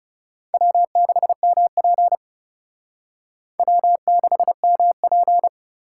Morse code, heard as a keyed tone, about 36 wpm, 700 Hz. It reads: W6MP W6MP
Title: Speed Racing - USA 1x2 Call Signs 30wpm